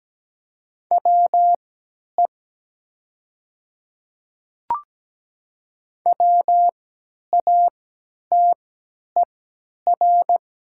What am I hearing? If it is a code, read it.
Morse code, heard as a keyed tone, about 17 wpm, spaced as for 10 wpm, 700 Hz. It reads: WE WATER
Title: 3x Character Spacing - Top 100 Words - 17wpm